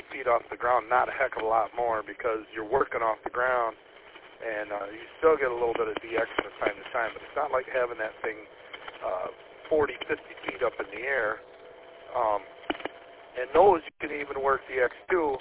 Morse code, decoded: none

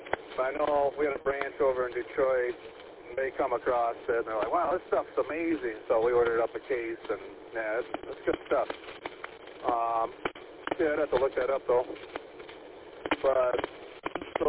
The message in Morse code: none

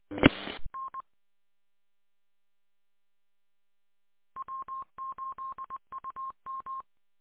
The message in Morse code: E N W8UM

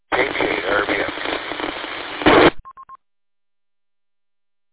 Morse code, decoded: none

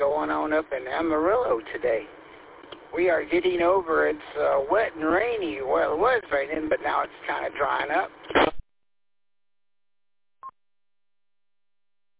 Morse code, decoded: none

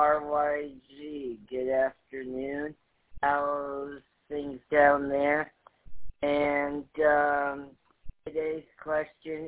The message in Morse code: none

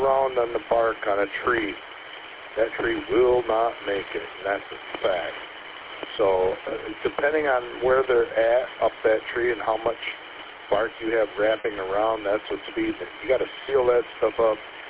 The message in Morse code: none

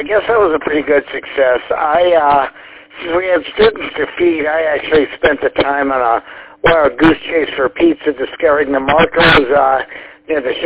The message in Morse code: none

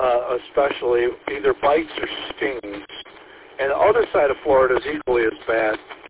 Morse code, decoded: none

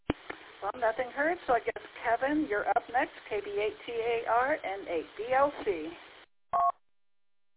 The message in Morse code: none

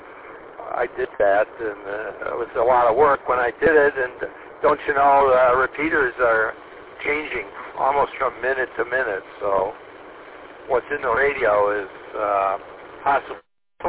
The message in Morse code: none

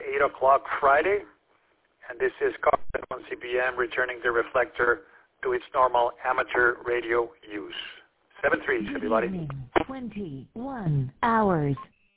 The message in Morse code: none

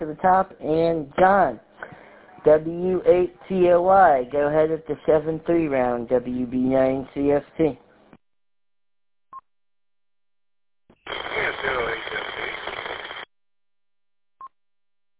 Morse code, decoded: none

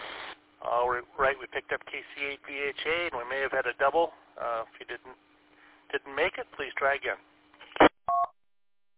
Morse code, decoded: none

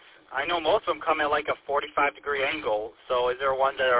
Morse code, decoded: none

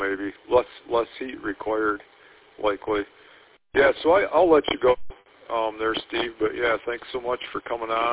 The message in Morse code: none

